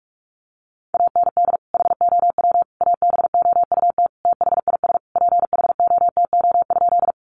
Morse code, heard as a keyed tone, about 46 wpm, 700 Hz. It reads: AND HOW ABOUT THIS PHOTO?
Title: Sentences from Top 1200 Words 45wpm